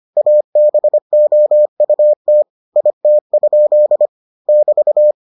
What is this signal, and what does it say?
Morse code, heard as a keyed tone, about 25 wpm, 600 Hz. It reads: ABOUT IT? <BT>